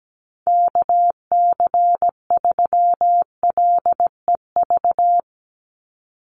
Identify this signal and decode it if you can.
Morse code, heard as a keyed tone, about 17 wpm, 700 Hz. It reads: KC3LEV
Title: USA Call Signs 17wpm